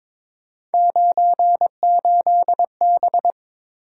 Morse code, heard as a keyed tone, about 22 wpm, 700 Hz. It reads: 98B
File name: practice / 22wpm / Three Letter-Number Combinations - 22wpm